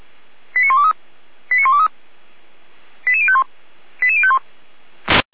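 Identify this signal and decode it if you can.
Five-tone selective call (ZVEI), heard as quick runs of five tones, tones of about 70 ms. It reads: 89123 89123 88051 88051
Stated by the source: ZVEI five-tone selective call 89770 received by SDR.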